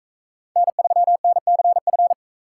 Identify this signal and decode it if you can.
Morse code, heard as a keyed tone, about 39 wpm, 700 Hz. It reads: N3NCF